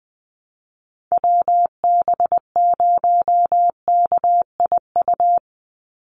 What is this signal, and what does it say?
Morse code, heard as a keyed tone, about 20 wpm, 690 Hz. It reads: WB0KIU
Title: NA SOTA Chaser Callsigns 20wpm